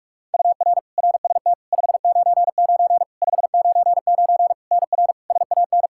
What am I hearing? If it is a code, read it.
Morse code, heard as a keyed tone, about 43 wpm, 700 Hz. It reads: UR RST 599 599 NR SAN
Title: Practice QSO - Exchange 1 - Beginning - 45wpm